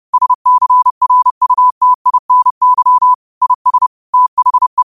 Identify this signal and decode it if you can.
Morse code, heard as a keyed tone, about 30 wpm, 1000 Hz. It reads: SCRUTINY IS THE
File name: qotdfiles / 30wpm